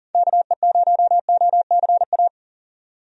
Morse code, heard as a keyed tone, about 40 wpm, 700 Hz. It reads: KE0OCA
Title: USA Call Signs - 6 Characters - 40wpm